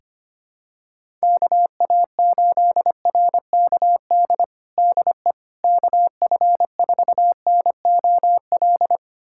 Morse code, heard as a keyed tone, about 25 wpm, 700 Hz. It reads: KA8RKD DE KF4NOL
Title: USA Callsign DE Callsign - 6 Characters - 25wpm